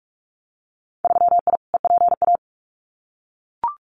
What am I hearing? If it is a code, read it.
Morse code, heard as a keyed tone, about 44 wpm, 690 Hz. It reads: 3I EPA
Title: Winter Field Day Exchanges Encore 45wpm